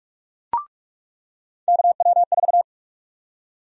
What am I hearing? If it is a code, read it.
Morse code, heard as a keyed tone, about 46 wpm, 700 Hz. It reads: KW4
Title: USA Call Sign Prefixes - 45wpm